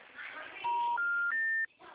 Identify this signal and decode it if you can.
Special information tone, with noise.